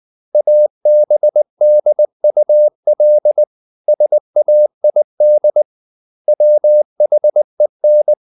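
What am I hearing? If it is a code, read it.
Morse code, heard as a keyed tone, about 20 wpm, 600 Hz. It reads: ABDUL SAID WHEN
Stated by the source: Morse code at 19 wpm from CW med Gnister, reading Ken Follett's novel Never in English.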